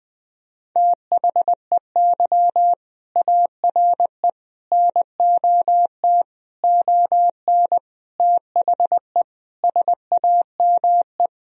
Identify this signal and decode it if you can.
Morse code, heard as a keyed tone, about 20 wpm, 700 Hz. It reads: THEY ARE NOT ON THE SAME
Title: Sentences from Top 1000 Words - Review Part 4 - 20wpm